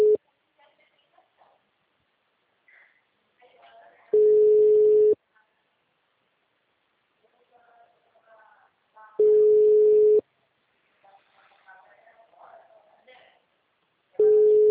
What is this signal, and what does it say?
Ringback tone (Europe, 425 Hz), with speech.
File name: out- -2011- - - .69507